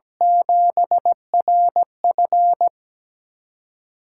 Morse code, heard as a keyed tone, about 17 wpm, 700 Hz. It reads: 7RF
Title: USA Call Signs - 8x Character Spacing - 17wpm